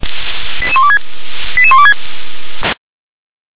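Five-tone selective call (ZVEI), heard as quick runs of five tones, tones of about 70 ms. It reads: 88126 88126
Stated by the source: ZVEI five-tone call received by SDR.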